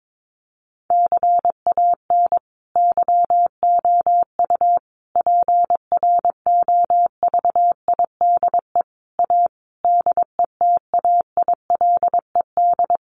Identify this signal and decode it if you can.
Morse code, heard as a keyed tone, about 22 wpm, 700 Hz. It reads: CAN YOU PROVIDE A DETAILED
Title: Sentences from Top 1500 Words 22wpm